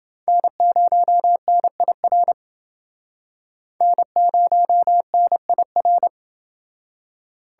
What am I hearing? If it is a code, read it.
Morse code, heard as a keyed tone, about 29 wpm, 700 Hz. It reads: N0NIR N0NIR